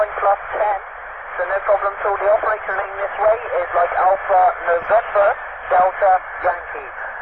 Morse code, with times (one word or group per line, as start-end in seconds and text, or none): none